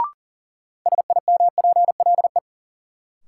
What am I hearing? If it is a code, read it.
Morse code, heard as a keyed tone, about 38 wpm, 700 Hz. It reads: SIMPLE
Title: Sentences from Top 400 Words - New Word Drill - 40wpm